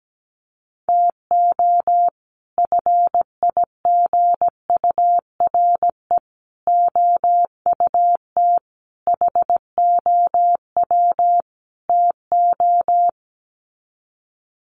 Morse code, heard as a keyed tone, about 17 wpm, 700 Hz. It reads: TO FIGURE OUT HOW TO
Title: Sets of 5 Words 17wpm